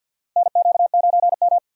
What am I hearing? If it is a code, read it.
Morse code, heard as a keyed tone, about 52 wpm, 700 Hz. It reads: NQ9M
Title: USA Call Signs - 4 Characters - 6x Character Spacing - 50wpm